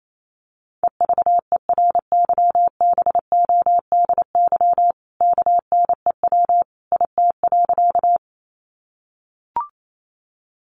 Morse code, heard as a keyed tone, about 28 wpm, 700 Hz. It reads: EVERYBODY KNEW IT.
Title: Sentences from Top 1400 Words 28wpm